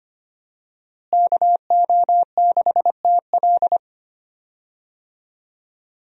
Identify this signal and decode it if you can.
Morse code, heard as a keyed tone, about 25 wpm, 700 Hz. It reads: KO6TL
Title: USA Call Signs - 5 Characters - Repeated 2x - 25wpm